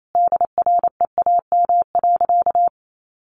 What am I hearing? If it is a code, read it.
Morse code, heard as a keyed tone, about 28 wpm, 700 Hz. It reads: DREAM.
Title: Sentences from Top 1500 Words 28wpm